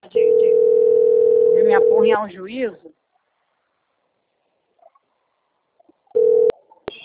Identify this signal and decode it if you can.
Ringback tone (North America), 440 + 480 Hz, with speech.
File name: out- - - - - .118